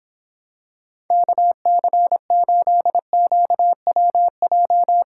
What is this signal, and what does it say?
Morse code, heard as a keyed tone, about 26 wpm, 700 Hz. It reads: KC8QWJ